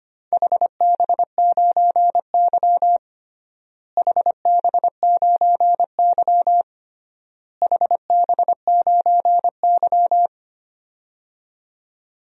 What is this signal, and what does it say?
Morse code, heard as a keyed tone, about 25 wpm, 700 Hz. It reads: HB9Y HB9Y HB9Y
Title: DXCC-100 Call Signs - 4 Characters - Repeated 3x - 25wpm